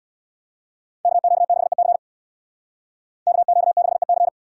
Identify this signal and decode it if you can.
Morse code, heard as a keyed tone, about 77 wpm, 700 Hz. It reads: KQ6EC KQ6EC